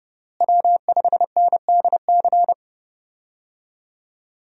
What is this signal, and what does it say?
Morse code, heard as a keyed tone, about 32 wpm, 700 Hz. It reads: W5NDC
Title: USA Call Signs - 5 Characters - Repeated 3x - 30wpm